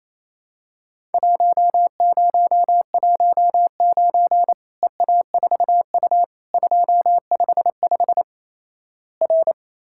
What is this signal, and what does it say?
Morse code, heard as a keyed tone, about 28 wpm, 700 Hz. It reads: 1019 EA4U 255 R
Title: Worked All Europe DX Contest - QTC Exchange - 28wpm